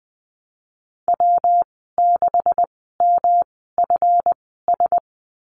Morse code, heard as a keyed tone, about 20 wpm, spaced as for 15 wpm, 700 Hz. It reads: W6MFS
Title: 140 Lesson 23g - Letter f - Call Signs 20wpm